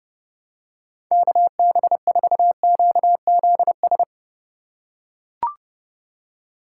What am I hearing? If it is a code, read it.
Morse code, heard as a keyed tone, about 28 wpm, 700 Hz. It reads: KB4QZS